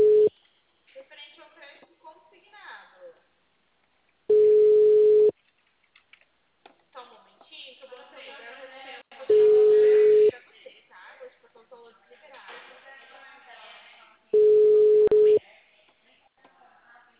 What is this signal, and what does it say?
Ringback tone (Europe, 425 Hz), with speech.